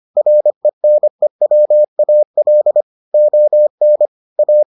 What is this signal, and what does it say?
Morse code, heard as a keyed tone, about 25 wpm, 600 Hz. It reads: RENEWAL ON A